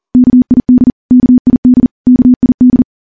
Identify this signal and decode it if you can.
Morse code, heard as a keyed tone, about 43 wpm, 250 Hz. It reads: KID KID KID
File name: morse / morse-training-40WPM